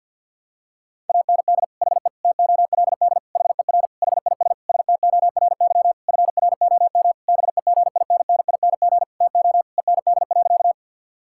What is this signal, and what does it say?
Morse code, heard as a keyed tone, about 52 wpm, 690 Hz. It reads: AND HE TOLD HER HIS STORY FROM BEGINNING TO END.
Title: Sentences from Top 1000 Words - Review Part 5 - 50wpm